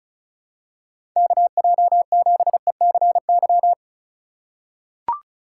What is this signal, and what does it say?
Morse code, heard as a keyed tone, about 36 wpm, 690 Hz. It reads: KJ7ECY
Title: USA Call Signs - 6 Characters - 4x Character Spacing - 35wpm